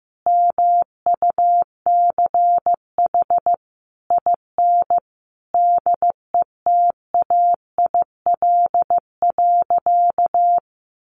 Morse code, heard as a keyed tone, about 15 wpm, 700 Hz. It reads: MUCH IN DETAIL.
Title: Sentences from Top 1500 Words 15wpm